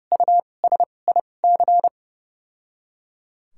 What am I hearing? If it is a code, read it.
Morse code, heard as a keyed tone, about 29 wpm, spaced as for 21 wpm, 700 Hz. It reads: USIC
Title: 2x Character Spacing - Top 300 Words - 30wpm